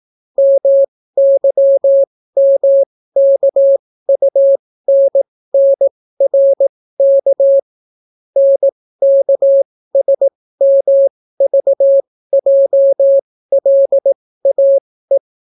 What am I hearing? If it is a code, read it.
Morse code, heard as a keyed tone, about 18 wpm, spaced as for 14 wpm, 550 Hz. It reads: MYMKUNNRK NKSMVJLAE